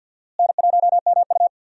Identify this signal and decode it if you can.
Morse code, heard as a keyed tone, about 52 wpm, 700 Hz. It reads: N1MU
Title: USA Call Signs - 4 Characters - Repeated 2x - 50wpm